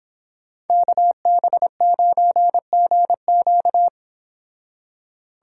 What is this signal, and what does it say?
Morse code, heard as a keyed tone, about 25 wpm, 700 Hz. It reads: KB9GQ